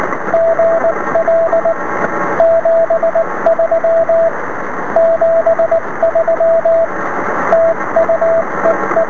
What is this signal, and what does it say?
Morse code, heard as a keyed tone, about 19 wpm, 660 Hz, with radio noise. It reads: GL 73 73 TU EE
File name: sm7lcb / es5pc 23cm